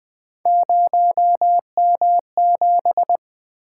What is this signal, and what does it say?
Morse code, heard as a keyed tone, about 20 wpm, 700 Hz. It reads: 0M7
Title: Three Letter-Number Combinations - 20wpm